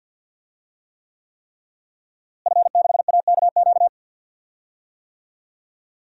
Morse code, heard as a keyed tone, about 46 wpm, 700 Hz. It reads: F6AKQ
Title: DXCC-100 Call Signs - 5 Characters - Repeated 3x - 50wpm